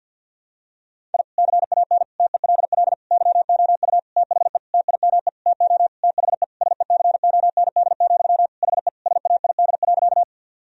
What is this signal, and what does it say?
Morse code, heard as a keyed tone, about 51 wpm, 700 Hz. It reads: I CAN TELL YOU THE TIME TO THE SECOND, HE SAID.
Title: Sentences from Top 1000 Words - Review Part 1 - 50wpm